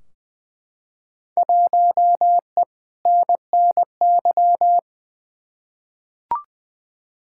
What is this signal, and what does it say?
Morse code, heard as a keyed tone, about 20 wpm, 700 Hz. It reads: 1E NNY
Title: Field Day Exchanges - 20wpm